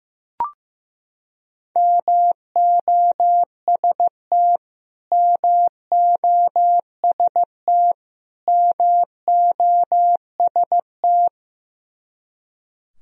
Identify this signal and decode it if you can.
Morse code, heard as a keyed tone, about 16 wpm, 700 Hz. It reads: MOST MOST MOST
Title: Top 100 Words Repeated 15wpm